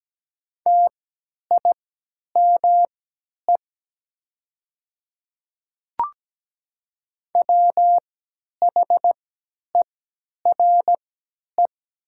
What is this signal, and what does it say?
Morse code, heard as a keyed tone, about 17 wpm, spaced as for 10 wpm, 700 Hz. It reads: TIME WHERE
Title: 3x Character Spacing - Top 200 Words - 17wpm